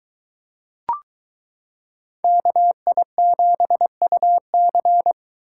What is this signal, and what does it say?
Morse code, heard as a keyed tone, about 22 wpm, 700 Hz. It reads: KI7UC